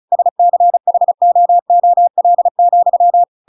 Morse code, heard as a keyed tone, about 34 wpm, 700 Hz. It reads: SCHOOL,
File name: Pinocchio / Pinocchio ch8 35 35 ik1hge